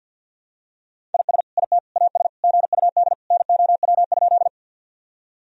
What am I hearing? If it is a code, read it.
Morse code, heard as a keyed tone, about 50 wpm, 700 Hz. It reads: IS IT AS GUD NOW?
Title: Sentences from Top 200 Words and Abbreviations 50wpm